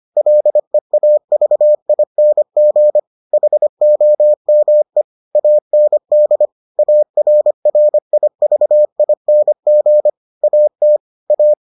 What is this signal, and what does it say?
Morse code, heard as a keyed tone, about 25 wpm, 600 Hz. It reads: LEAVING HOME AND ARRIVING AT A